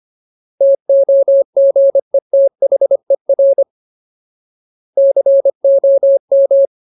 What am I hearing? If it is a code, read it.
Morse code, heard as a keyed tone, about 25 wpm, 550 Hz. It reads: TOGETHER COM
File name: Ultimate Word List / 25WPM Ultimate